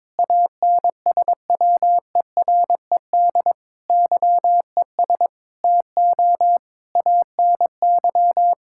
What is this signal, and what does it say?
Morse code, heard as a keyed tone, about 22 wpm, 700 Hz. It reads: ANSWERED YES TO ANY
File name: Sentences with 9-Word Phrases - 22wpm